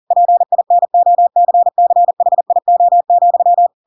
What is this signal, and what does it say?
Morse code, heard as a keyed tone, about 40 wpm, 700 Hz. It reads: PINOCCHIO,
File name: Pinocchio / Pinocchio ch4 40 40 ik1hge